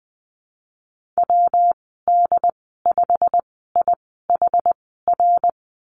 Morse code, heard as a keyed tone, about 20 wpm, spaced as for 15 wpm, 700 Hz. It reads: WD5IHR
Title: 179 Lesson 29g - Slash - Call Signs 20wpm